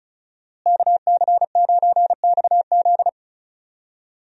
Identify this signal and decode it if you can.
Morse code, heard as a keyed tone, about 34 wpm, 700 Hz. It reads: KC9XZ
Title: USA Call Signs - 5 Characters - Repeated 2x - 35wpm